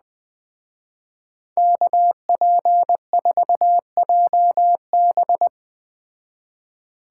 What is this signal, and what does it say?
Morse code, heard as a keyed tone, about 20 wpm, 700 Hz. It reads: KP4JB
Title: Speed Racing - USA 2x2 Call Signs 17wpm